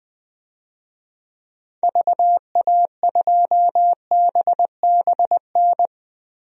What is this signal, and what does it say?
Morse code, heard as a keyed tone, about 20 wpm, 700 Hz. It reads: VA2BBN
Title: DXCC-100 Call Signs - 6 Characters - 20wpm